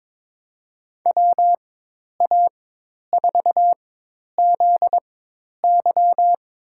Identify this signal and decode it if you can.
Morse code, heard as a keyed tone, about 22 wpm, spaced as for 10 wpm, 700 Hz. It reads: WA4ZY
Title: USA Call Signs - 4x Character Spacing - 22wpm